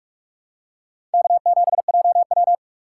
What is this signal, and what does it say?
Morse code, heard as a keyed tone, about 44 wpm, 700 Hz. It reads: K7JW
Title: USA Call Signs - 4 Characters - 2x Character Spacing - 45wpm